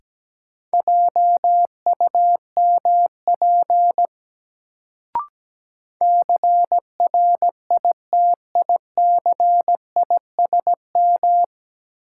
Morse code, heard as a keyed tone, about 17 wpm, 700 Hz. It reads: JUMP CRITICISM
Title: Sentences from Top 1300 Words - New Word Drill - 17wpm